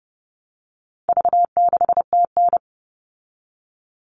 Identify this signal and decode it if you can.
Morse code, heard as a keyed tone, about 30 wpm, 700 Hz. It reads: V6TN
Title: All Cut Numbers - 4-Digit 30wpm